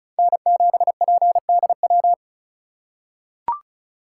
Morse code, heard as a keyed tone, about 34 wpm, 700 Hz. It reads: N7PDW
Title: USA Call Signs - 3x Character Spacing - 35wpm